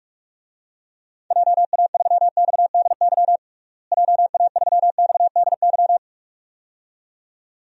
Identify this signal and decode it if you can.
Morse code, heard as a keyed tone, about 46 wpm, 700 Hz. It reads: JA3XDY JA3XDY
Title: DXCC-100 Call Signs - 6 Characters - Repeated 2x - 45wpm